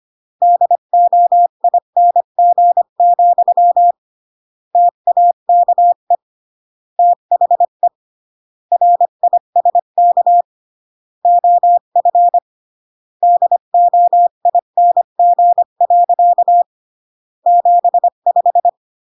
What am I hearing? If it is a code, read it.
Morse code, heard as a keyed tone, about 25 wpm, 700 Hz. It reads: DOING, TAKE THE RISK OF DOING. 75